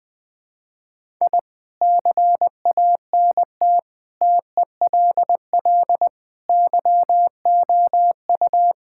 Morse code, heard as a keyed tone, about 20 wpm, 700 Hz. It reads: I CANT TELL YOU
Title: Sentences with 8-Word Phrases - Spoken Beginning - 20wpm